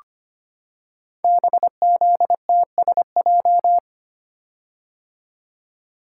Morse code, heard as a keyed tone, about 25 wpm, 700 Hz. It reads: BZTSJ
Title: Code Groups - 5 Letters - 25wpm